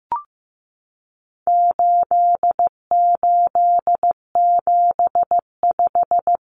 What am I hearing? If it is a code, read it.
Morse code, heard as a keyed tone, about 15 wpm, 700 Hz. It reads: 8875
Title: Common Cut Numbers - 4-Digit 15wpm